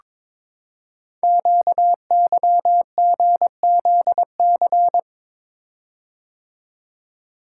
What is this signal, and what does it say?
Morse code, heard as a keyed tone, about 23 wpm, 700 Hz. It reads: QYGZC
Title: Code Groups - 5 Letters - 22wpm